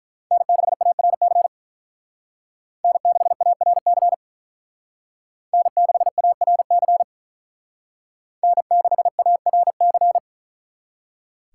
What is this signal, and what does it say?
Morse code, heard as a keyed tone, about 45 wpm, 700 Hz. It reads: N6ARC N6ARC N6ARC N6ARC